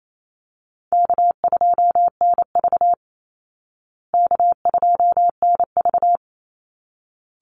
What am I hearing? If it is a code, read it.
Morse code, heard as a keyed tone, about 28 wpm, 700 Hz. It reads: K2NV K2NV
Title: Call Signs Repeated 3x - 28wpm